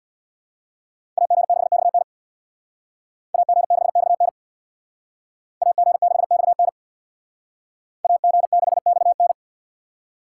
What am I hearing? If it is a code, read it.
Morse code, heard as a keyed tone, about 64 wpm, 700 Hz. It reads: AG6XN AG6XN AG6XN AG6XN